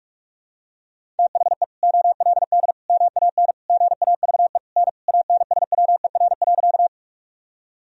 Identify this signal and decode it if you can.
Morse code, heard as a keyed tone, about 46 wpm, 700 Hz. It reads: THE OLD MAN GAVE N ANSWER.
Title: Sentences from Top 500 Words and Abbreviations 45wpm